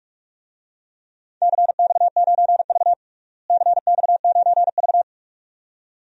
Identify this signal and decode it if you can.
Morse code, heard as a keyed tone, about 46 wpm, 700 Hz. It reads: CX9V CX9V
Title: DXCC-100 Call Signs - 4 Characters - Repeated 2x - 45wpm